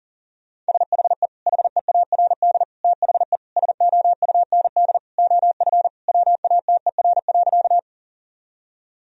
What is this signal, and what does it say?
Morse code, heard as a keyed tone, about 40 wpm, 700 Hz. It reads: SHE HEARD THE SOUND OF WATER.
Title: Sentences from Top 1000 Words - Review Part 1 - 40wpm